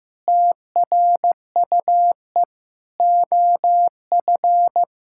Morse code, heard as a keyed tone, about 16 wpm, 700 Hz. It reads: TRUE OF
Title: Sentences from Top 1000 Words - Review Part 6 - 15wpm